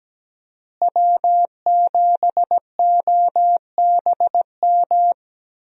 Morse code, heard as a keyed tone, about 17 wpm, 700 Hz. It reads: W7OBM
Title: USA Call Signs - 5 Characters - Repeated 2x - 17wpm